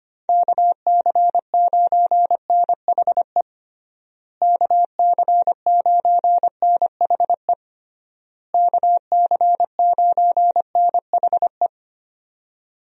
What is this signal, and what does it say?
Morse code, heard as a keyed tone, about 25 wpm, 700 Hz. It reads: KC9NHE KC9NHE KC9NHE
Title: USA Call Signs - 6 Characters - Repeated 3x - 25wpm